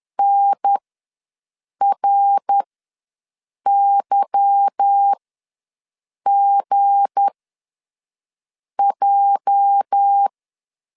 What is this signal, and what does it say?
Morse code, heard as a keyed tone, about 10 wpm, spaced as for 5 wpm, 800 Hz. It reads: NRYGJ
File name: LEZIONI / 8 YJ45 / 8 YJ ws 10wpm 4043